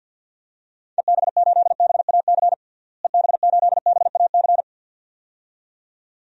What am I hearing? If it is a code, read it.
Morse code, heard as a keyed tone, about 53 wpm, 700 Hz. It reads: EB8BAC EB8BAC
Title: DXCC-100 Call Signs - 6 Characters - Repeated 2x - 50wpm